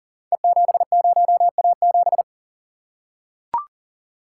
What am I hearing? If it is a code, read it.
Morse code, heard as a keyed tone, about 38 wpm, 690 Hz. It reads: E70A7
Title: Code Groups - 5 Letter:Numbers - 40wpm